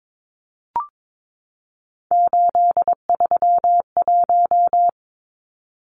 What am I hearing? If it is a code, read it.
Morse code, heard as a keyed tone, about 22 wpm, 700 Hz. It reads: 831
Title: Numbers - 3-Digit 22wpm